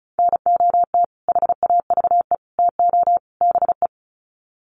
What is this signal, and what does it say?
Morse code, heard as a keyed tone, about 34 wpm, 700 Hz. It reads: NOT HAVE TO BE
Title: Sets of 4 Words 35wpm